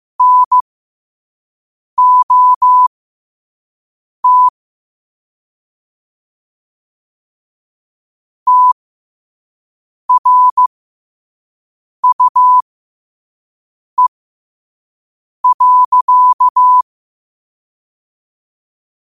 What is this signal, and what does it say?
Morse code, heard as a keyed tone, about 15 wpm, spaced as for 5 wpm, 1000 Hz. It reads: NOT TRUE.
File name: qotdfiles / 5wpm